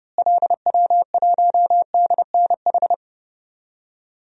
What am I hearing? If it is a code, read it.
Morse code, heard as a keyed tone, about 30 wpm, 700 Hz. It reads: LW1DNH
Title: DXCC-100 Call Signs - 6 Characters - Repeated 2x - 30wpm